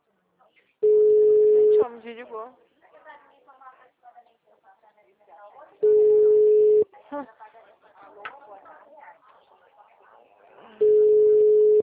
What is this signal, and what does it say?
Ringback tone (Europe, 425 Hz), with speech.